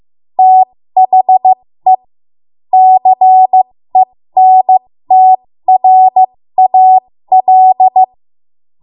Morse code, heard as a keyed tone, about 15 wpm, spaced as for 13 wpm, 750 Hz. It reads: THE CENTRAL
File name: Morse / 13WPM